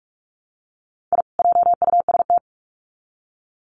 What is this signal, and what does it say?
Morse code, heard as a keyed tone, about 44 wpm, 700 Hz. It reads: I JUST
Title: Sets of 2 Words 45wpm